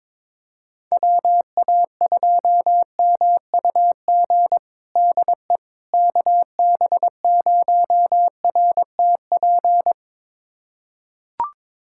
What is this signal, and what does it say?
Morse code, heard as a keyed tone, about 21 wpm, 700 Hz. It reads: WA2MUG DE KB0RTP
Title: USA Callsign DE Callsign - 6 Characters - 22wpm